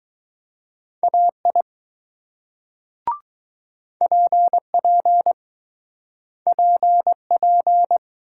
Morse code, heard as a keyed tone, about 20 wpm, 700 Hz. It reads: AI PP PP